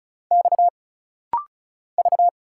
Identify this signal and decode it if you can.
Morse code, heard as a keyed tone, about 36 wpm, 700 Hz. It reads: X V
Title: Single Letter-Number Alt - Rapid-Fire 35wpm